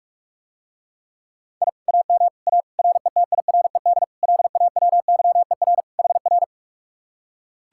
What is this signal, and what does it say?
Morse code, heard as a keyed tone, about 45 wpm, 700 Hz. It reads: I AM A RETIRED LAWYER HR